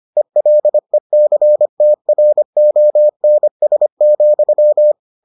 Morse code, heard as a keyed tone, about 25 wpm, 600 Hz. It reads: ELECTRONS,